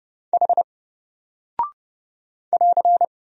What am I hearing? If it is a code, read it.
Morse code, heard as a keyed tone, about 30 wpm, 700 Hz. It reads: H <AR>
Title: Single Character (Letter:Number:Punctuation:Prosign) 30wpm